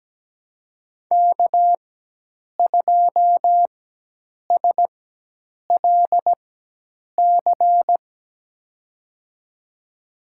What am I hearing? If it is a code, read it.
Morse code, heard as a keyed tone, about 17 wpm, spaced as for 8 wpm, 700 Hz. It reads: K2SLC
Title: USA Call Signs - 5 Characters - 4x Character Spacing - 17wpm